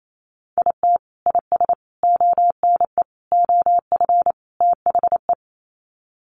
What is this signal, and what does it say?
Morse code, heard as a keyed tone, about 28 wpm, 700 Hz. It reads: IT IS ONE OF THE
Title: Sets of 5 Words 28wpm